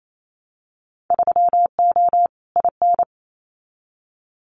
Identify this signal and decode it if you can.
Morse code, heard as a keyed tone, about 28 wpm, 700 Hz. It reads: 3O IN
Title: Winter Field Day Exchanges Encore 28wpm